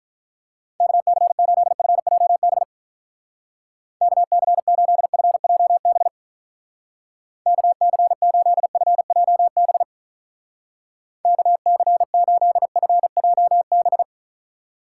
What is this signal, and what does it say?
Morse code, heard as a keyed tone, about 43 wpm, 700 Hz. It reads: KC8FJB KC8FJB KC8FJB KC8FJB
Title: Speed Racing - USA 2x3 Call Signs 35wpm